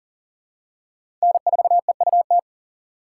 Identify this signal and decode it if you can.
Morse code, heard as a keyed tone, about 40 wpm, 700 Hz. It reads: N4EUT
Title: USA Call Signs - 5 Characters - 6x Character Spacing - 40wpm